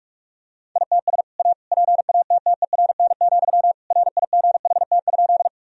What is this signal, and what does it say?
Morse code, heard as a keyed tone, about 41 wpm, 700 Hz. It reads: ITS A PATTERN, RIGHT?
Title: Sentences from Top 1000 Words - Review Part 2 - 45wpm